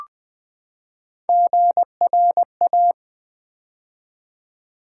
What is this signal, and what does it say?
Morse code, heard as a keyed tone, about 20 wpm, 700 Hz. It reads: GRA
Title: Build A Word - 3 Letters At A Time - 8-Letter Words - 20wpm